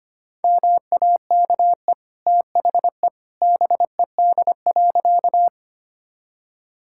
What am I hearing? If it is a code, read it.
Morse code, heard as a keyed tone, about 25 wpm, 700 Hz. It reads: MAKE THE BED.
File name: Easy Sentences with Hints - 25wpm